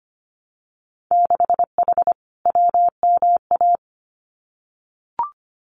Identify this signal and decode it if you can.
Morse code, heard as a keyed tone, about 25 wpm, 700 Hz. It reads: 6H WMA
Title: Winter Field Day Exchanges Encore 25wpm